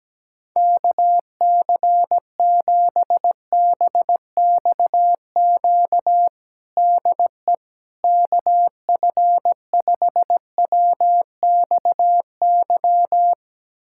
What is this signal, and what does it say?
Morse code, heard as a keyed tone, about 17 wpm, 700 Hz. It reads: KC7BXQ DE KF5WXY
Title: USA Callsign DE Callsign 17wpm